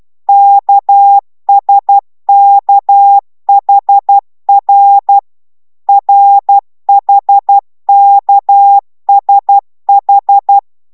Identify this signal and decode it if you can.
Morse code, heard as a keyed tone, about 12 wpm, 800 Hz. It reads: KSKHR RHKSH